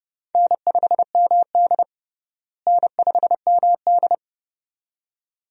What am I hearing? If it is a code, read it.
Morse code, heard as a keyed tone, about 30 wpm, 700 Hz. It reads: N5MD N5MD
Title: USA Call Signs - 4 Characters - Repeated 2x - 30wpm